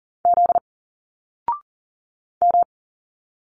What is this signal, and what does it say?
Morse code, heard as a keyed tone, about 40 wpm, 700 Hz. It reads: Z M
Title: Single Letters 40wpm